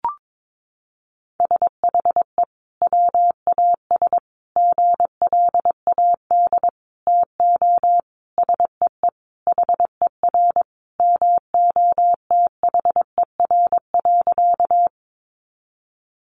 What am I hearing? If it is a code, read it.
Morse code, heard as a keyed tone, about 22 wpm, 700 Hz. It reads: SHE WAS GLAD TO SEE HER MOTHER.
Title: Sentences from Top 800 Words 22wpm